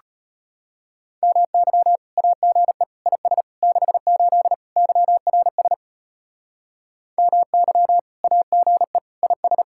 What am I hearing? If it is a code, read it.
Morse code, heard as a keyed tone, about 31 wpm, 700 Hz. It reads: MY AGE IS 68 YRS MY AGE IS
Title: Speed-Racing Common QSO Phrases - 25wpm